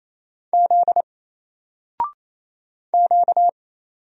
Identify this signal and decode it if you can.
Morse code, heard as a keyed tone, about 27 wpm, 700 Hz. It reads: Z Q
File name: Single-Letter-Number-28wpm